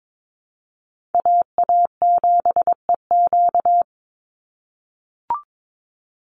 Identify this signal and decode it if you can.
Morse code, heard as a keyed tone, about 22 wpm, 700 Hz. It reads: AA7EQ
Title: NA SOTA Chaser Callsigns 22wpm